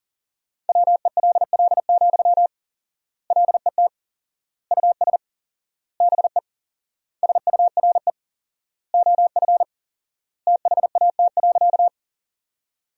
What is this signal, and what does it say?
Morse code, heard as a keyed tone, about 40 wpm, 700 Hz. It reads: WELL, LET US BE SURE OF THAT.
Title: 4x Word-Spacing - Sentences from Top 300 Words - 40wpm